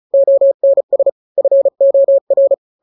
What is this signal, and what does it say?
Morse code, heard as a keyed tone, about 34 wpm, 550 Hz. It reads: ONS FOR
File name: 3-grams 35 30 WPM